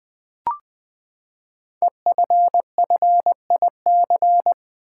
Morse code, heard as a keyed tone, about 20 wpm, 700 Hz. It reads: EFFIC
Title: Build A Word - 5 Letters At A Time - 10-Letter Words - 20wpm